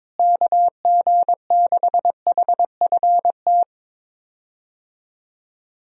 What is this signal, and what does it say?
Morse code, heard as a keyed tone, about 22 wpm, 700 Hz. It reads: KG6HFT
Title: USA Call Signs - 6 Characters - Repeated 3x - 22wpm